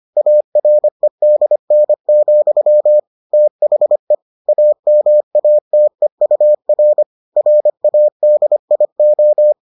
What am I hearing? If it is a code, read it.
Morse code, heard as a keyed tone, about 25 wpm, 600 Hz. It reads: AREDN, THE AMATEUR RADIO